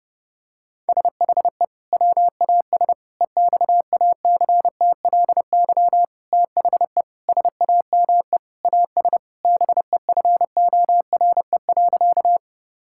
Morse code, heard as a keyed tone, about 30 wpm, 700 Hz. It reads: SHE WAS EXACTLY THE SAME AS BEFORE.
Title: Sentences from Top 1000 Words - Review Part 6 - 30wpm